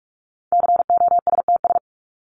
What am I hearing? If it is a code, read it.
Morse code, heard as a keyed tone, about 44 wpm, 700 Hz. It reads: COSTS
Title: ES:S Suffix - Top 100 Words with Base Words 45wpm